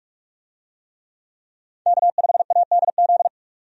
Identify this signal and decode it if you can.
Morse code, heard as a keyed tone, about 47 wpm, 700 Hz. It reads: K5ADZ
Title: USA Call Signs - 5 Characters - 45wpm